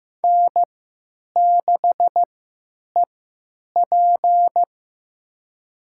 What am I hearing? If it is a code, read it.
Morse code, heard as a keyed tone, about 15 wpm, spaced as for 8 wpm, 700 Hz. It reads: N6EP
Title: USA Call Signs - 3x Character Spacing - 15wpm